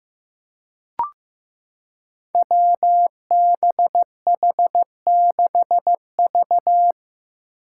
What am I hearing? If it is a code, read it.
Morse code, heard as a keyed tone, about 16 wpm, 700 Hz. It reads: WBH6V
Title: Five Challenging Letter-Number Combinations 15wpm